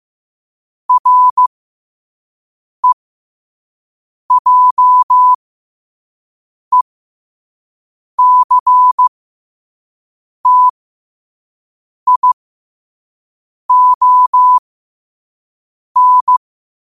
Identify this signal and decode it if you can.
Morse code, heard as a keyed tone, about 15 wpm, spaced as for 5 wpm, 1000 Hz. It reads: REJECTION
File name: qotdfiles / 5wpm